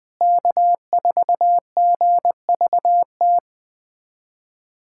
Morse code, heard as a keyed tone, about 20 wpm, 700 Hz. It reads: K4GVT
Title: USA Callsign DE Callsign - 5 Characters - 20wpm